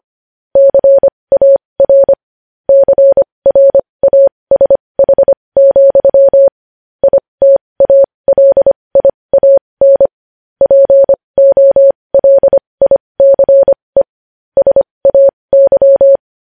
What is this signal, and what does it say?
Morse code, heard as a keyed tone, about 25 wpm, spaced as for 20 wpm, 550 Hz. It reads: CAR CRASH, ITALIAN POLICE SAY
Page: news-headlines-in-morse-code-at-20-wpm